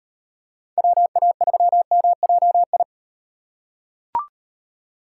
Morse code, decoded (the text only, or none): WA3MJI